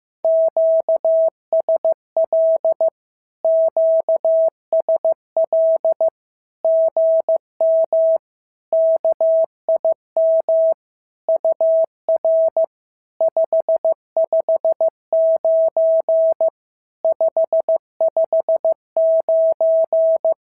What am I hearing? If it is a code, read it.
Morse code, heard as a keyed tone, about 15 wpm, 650 Hz. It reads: QSL QSL GM KIM UR 559 559
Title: SOTA QSO Practice 15wpm